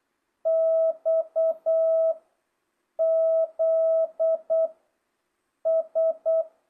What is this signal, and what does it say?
Morse code, heard as a keyed tone, about 8 wpm, 660 Hz. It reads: X Z S
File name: jh8cbh / 2025cw